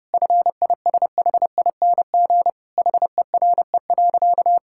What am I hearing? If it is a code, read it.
Morse code, heard as a keyed tone, about 32 wpm, 700 Hz. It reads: FISHING HERE.
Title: Sentences from Top 1000 Words - Review Part 5 - 30wpm